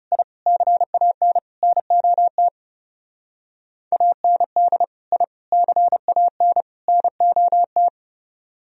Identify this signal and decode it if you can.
Morse code, heard as a keyed tone, about 37 wpm, 700 Hz. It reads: I CAN NOT AND I CAN NOT
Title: Speed-Racing - Sets of 4 Words 30wpm